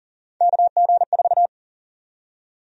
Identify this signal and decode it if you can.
Morse code, heard as a keyed tone, about 40 wpm, 700 Hz. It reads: KG4
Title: USA Call Sign Prefixes - 40wpm